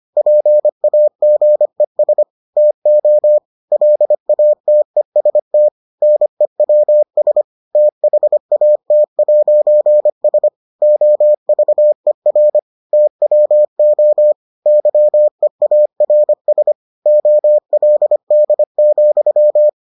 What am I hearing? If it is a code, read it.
Morse code, heard as a keyed tone, about 25 wpm, 600 Hz. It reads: PAGES TO LATEST NEWS THAT'S OVER TWO YEARS OLD,